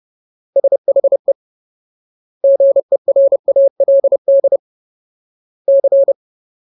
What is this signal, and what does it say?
Morse code, heard as a keyed tone, about 30 wpm, 550 Hz. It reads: SHE GERALD C
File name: Ultimate Word List / 30WPM Ultimate